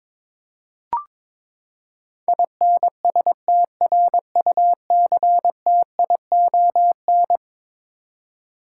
Morse code, heard as a keyed tone, about 22 wpm, 700 Hz. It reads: INSTRUCTION
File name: practice / 22wpm / Eleven Letter Words - 22wpm